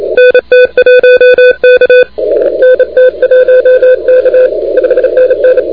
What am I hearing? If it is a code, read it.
Morse code, heard as a keyed tone, about 29 wpm, 510 Hz. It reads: NT1K NT1K 5NN
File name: recordings / DK8ZZ de NT1K ARRL10M CW 28MHz